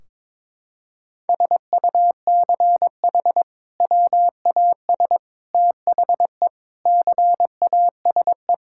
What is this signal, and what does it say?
Morse code, heard as a keyed tone, about 22 wpm, 700 Hz. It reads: SUCH WAS THE CASE
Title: Sentences from Top 1600 Words 22wpm